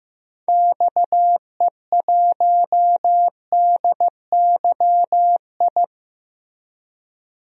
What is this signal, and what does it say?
Morse code, heard as a keyed tone, about 15 wpm, 700 Hz. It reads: XE1DYI
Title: DXCC-100 Call Signs - 6 Characters - Repeated 3x - 15wpm